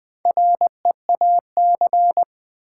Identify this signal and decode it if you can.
Morse code, heard as a keyed tone, about 20 wpm, 700 Hz. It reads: REAC
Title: Build A Word - 4 Letters At A Time - 8-Letter Words - 20wpm